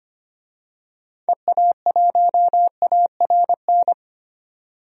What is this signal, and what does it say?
Morse code, heard as a keyed tone, about 25 wpm, 700 Hz. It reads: EA1ARN
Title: DXCC-100 Call Signs - 6 Characters - 25wpm